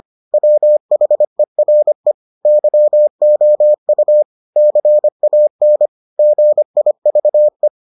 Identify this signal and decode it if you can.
Morse code, heard as a keyed tone, about 25 wpm, 600 Hz. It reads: WHERE YOU CAN GIVE